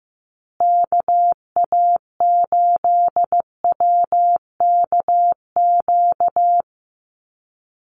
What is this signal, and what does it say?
Morse code, heard as a keyed tone, about 15 wpm, 700 Hz. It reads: KA8WKQ
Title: USA Call Signs Repeated 2x 15wpm